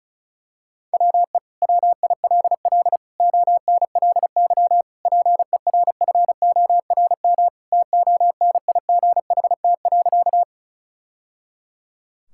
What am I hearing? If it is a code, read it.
Morse code, heard as a keyed tone, about 34 wpm, 700 Hz. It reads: WE WILL ONLY PERFORM TONIGHT.
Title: Sentences from Top 1000 Words - Review Part 5 - 35wpm